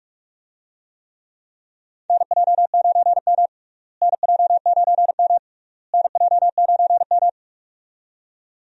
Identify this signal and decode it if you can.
Morse code, heard as a keyed tone, about 43 wpm, 700 Hz. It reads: NJ9M NJ9M NJ9M
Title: USA Call Signs - 4 Characters - Repeated 3x - 45wpm